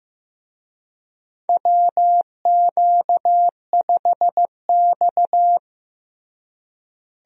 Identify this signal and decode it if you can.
Morse code, heard as a keyed tone, about 15 wpm, 700 Hz. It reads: WQ5X
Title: USA Call Signs - 4 Characters - Repeated 2x - 15wpm